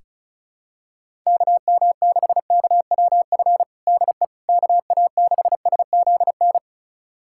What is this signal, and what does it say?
Morse code, heard as a keyed tone, about 36 wpm, 700 Hz. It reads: KM6KWF DE KA6SZN
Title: USA Callsign DE Callsign - 6 Characters - 35wpm